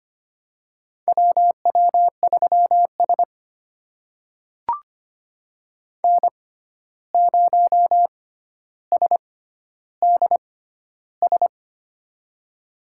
Morse code, heard as a keyed tone, about 25 wpm, spaced as for 12 wpm, 700 Hz. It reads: WW3S N0SDS
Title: USA Call Signs - 6x Character Spacing - 25wpm